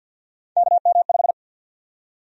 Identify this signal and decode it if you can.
Morse code, heard as a keyed tone, about 53 wpm, 700 Hz. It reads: KM5
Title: USA Call Sign Prefixes - Repeated 2x - 50wpm